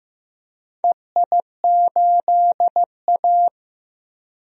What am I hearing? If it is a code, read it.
Morse code, heard as a keyed tone, about 15 wpm, 700 Hz. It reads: EI8A
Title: DXCC-100 Call Signs - 4 Characters - 15wpm